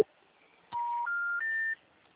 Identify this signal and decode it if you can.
Special information tone.